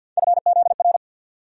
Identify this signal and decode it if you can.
Morse code, heard as a keyed tone, about 52 wpm, 700 Hz. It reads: FZR
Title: USA Call Sign Postfixes - 3 Letters 50wpm